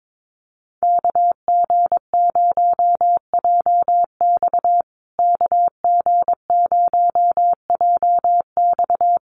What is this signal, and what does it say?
Morse code, heard as a keyed tone, about 22 wpm, 700 Hz. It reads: KG0JX KG0JX
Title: USA Call Signs Repeated 3x 22wpm